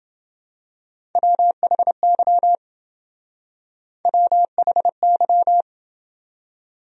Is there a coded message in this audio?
Yes: Morse code from 1.2 to 5.6 s, heard as a keyed tone, about 28 wpm, 700 Hz. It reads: WHY WHY